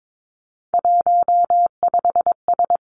Morse code, heard as a keyed tone, about 22 wpm, 700 Hz. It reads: 15S
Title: Three Challenging Letter-Number Combinations 22wpm